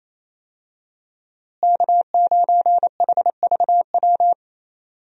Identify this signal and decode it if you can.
Morse code, heard as a keyed tone, about 27 wpm, 700 Hz. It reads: K9HVW
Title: USA Call Signs - 5 Characters - Repeated 2x - 28wpm